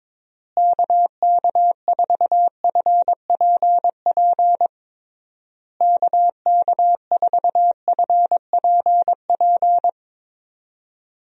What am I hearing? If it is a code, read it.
Morse code, heard as a keyed tone, about 22 wpm, 700 Hz. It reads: KK4FPP KK4FPP
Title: USA Call Signs - 6 Characters - Repeated 3x - 22wpm